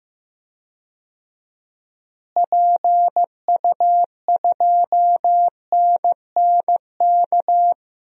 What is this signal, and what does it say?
Morse code, heard as a keyed tone, about 15 wpm, 700 Hz. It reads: PU2NNK
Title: DXCC-100 Call Signs - 6 Characters - Repeated 2x - 15wpm